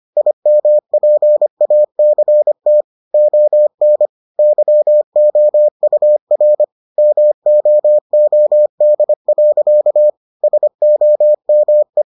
Morse code, heard as a keyed tone, about 25 wpm, 600 Hz. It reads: IMPACT ON YOUR MOOD. SOME